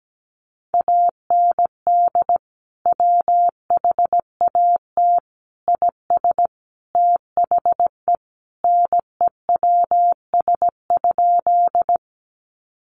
Morse code, heard as a keyed tone, about 17 wpm, 700 Hz. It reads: AND WHAT IS THE NEWS?
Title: Sentences from Top 800 Words 17wpm